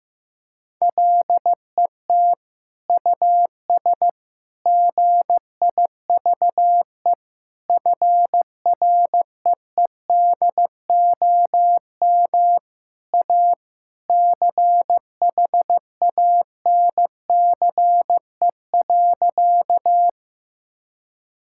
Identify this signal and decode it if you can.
Morse code, heard as a keyed tone, about 15 wpm, 700 Hz. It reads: LET US GIVE FREEDOM A CHANCE.
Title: Sentences from Top 1000 Words - Review Part 5 - 15wpm